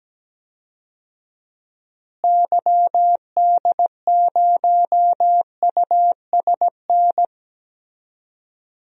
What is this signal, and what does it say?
Morse code, heard as a keyed tone, about 17 wpm, 700 Hz. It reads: YD0USN
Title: DXCC-100 Call Signs - 6 Characters - Repeated 3x - 17wpm